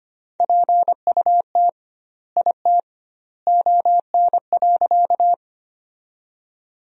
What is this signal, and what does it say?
Morse code, heard as a keyed tone, about 25 wpm, 700 Hz. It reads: PUT IT ON.
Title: 2x Word-Spacing - Sentences from Top 200 Words - 25wpm